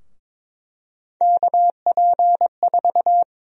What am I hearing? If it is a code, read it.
Morse code, heard as a keyed tone, about 22 wpm, 700 Hz. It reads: KP4
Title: USA Call Sign Prefixes - 22wpm